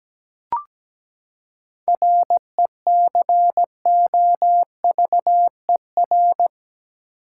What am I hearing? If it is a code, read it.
Morse code, heard as a keyed tone, about 17 wpm, 700 Hz. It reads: RECOVER
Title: Seven Letter Words - 17wpm